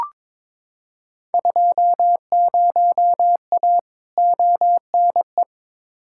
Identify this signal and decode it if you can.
Morse code, heard as a keyed tone, about 22 wpm, 700 Hz. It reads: 20A ONE
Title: Field Day Exchanges Encore - 22wpm